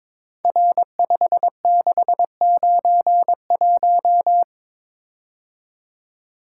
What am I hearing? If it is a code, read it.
Morse code, heard as a keyed tone, about 23 wpm, 700 Hz. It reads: R5691
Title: Code Groups - 5 Letter:Numbers - 22wpm